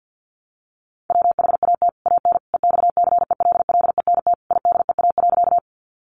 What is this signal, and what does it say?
Morse code, heard as a keyed tone, about 50 wpm, 700 Hz. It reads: WHAT AN EXCELLENT IDEA.
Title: Sentences from Top 1000 Words 50wpm